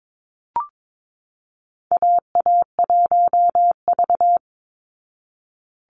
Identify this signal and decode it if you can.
Morse code, heard as a keyed tone, about 22 wpm, 700 Hz. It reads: AA1V
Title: Call Signs Repeated 3x - 22wpm